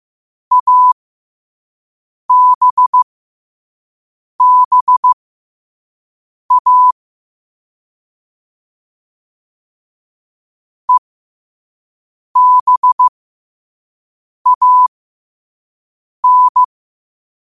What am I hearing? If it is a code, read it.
Morse code, heard as a keyed tone, about 15 wpm, spaced as for 5 wpm, 1000 Hz. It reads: ABBA EBAN
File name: 5wpm